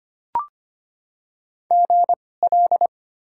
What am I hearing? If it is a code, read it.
Morse code, heard as a keyed tone, about 25 wpm, spaced as for 18 wpm, 700 Hz. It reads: GL
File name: Top 90 QSO Elements - 2x Character Spacing - 25wpm